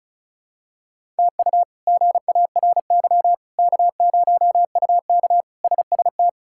Morse code, heard as a keyed tone, about 34 wpm, 700 Hz. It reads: TU GARY K0UK SST
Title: K1USN SST QSO Practice - 35wpm